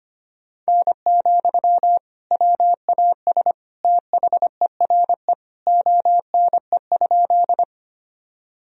Morse code, heard as a keyed tone, about 25 wpm, 700 Hz. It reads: N, WAS THERE ONE?
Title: Sentences from Top 100 Words and Abbreviations 25wpm